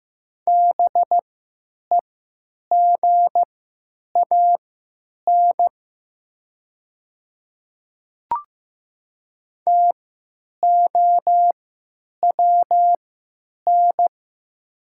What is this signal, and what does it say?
Morse code, heard as a keyed tone, about 15 wpm, spaced as for 9 wpm, 700 Hz. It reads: BEGAN TOWN
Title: 3x Character Spacing - Top 500 Words - 15wpm